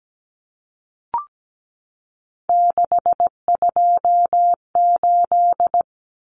Morse code, heard as a keyed tone, about 17 wpm, 700 Hz. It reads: 628
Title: Numbers - 3-Digit 17wpm